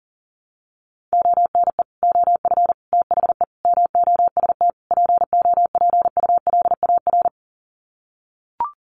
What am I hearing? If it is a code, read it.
Morse code, heard as a keyed tone, about 40 wpm, 700 Hz. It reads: ONE OF THE MOST POPULAR
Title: Sets of 5 Words 40wpm